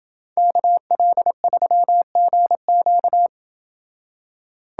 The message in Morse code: KL3GQ

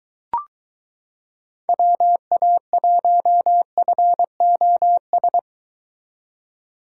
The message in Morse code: WA1FOS

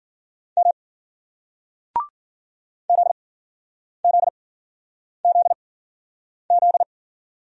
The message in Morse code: M Z Z Z Z

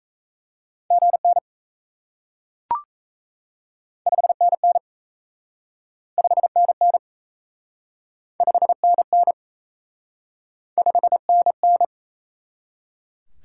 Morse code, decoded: GN 5NN 5NN 5NN 5NN

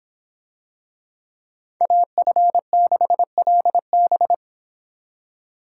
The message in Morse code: AF6LB

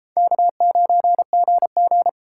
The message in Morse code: K9GG

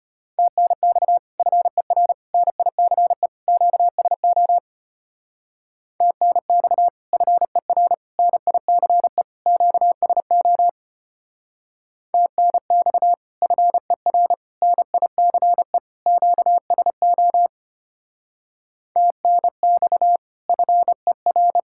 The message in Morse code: TNX FER NICE QSO TNX FER NICE QSO TNX FER NICE QSO TNX FER

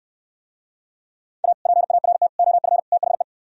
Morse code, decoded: A PART OF THE

